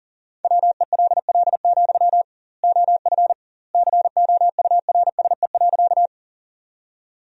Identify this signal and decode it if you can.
Morse code, heard as a keyed tone, about 40 wpm, 700 Hz. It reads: WELL, OF COURSE.
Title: 2x Word-Spacing - Sentences from Top 500 Words - 40wpm